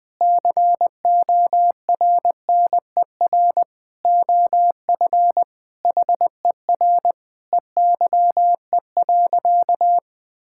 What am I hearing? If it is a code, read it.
Morse code, heard as a keyed tone, about 20 wpm, 700 Hz. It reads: CORNER OF HER EYE.
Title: Sentences with 5-Word Phrases - 20wpm